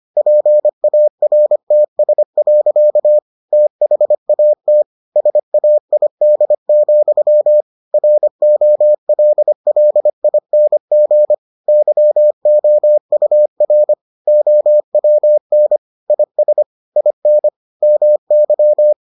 Morse code, heard as a keyed tone, about 25 wpm, 600 Hz. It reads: PARTS. THAT SAID, ROLLING YOUR OWN IS IN MY